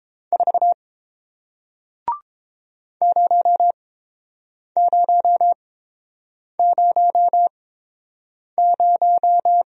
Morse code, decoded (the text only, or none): 4 0000